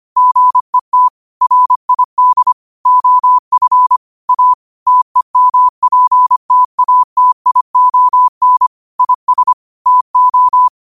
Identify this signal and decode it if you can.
Morse code, heard as a keyed tone, about 25 wpm, 1000 Hz. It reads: GET RID OF A TEMPTATION IS TO